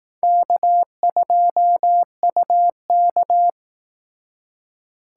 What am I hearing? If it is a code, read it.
Morse code, heard as a keyed tone, about 18 wpm, 700 Hz. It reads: K2UK
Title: Speed Racing - USA 1x2 Call Signs 15wpm